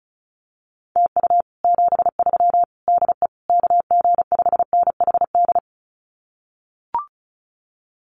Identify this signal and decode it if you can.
Morse code, heard as a keyed tone, about 36 wpm, 700 Hz. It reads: TU 73 DE KG5NHD
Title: Common QSO Phrases Encore 35wpm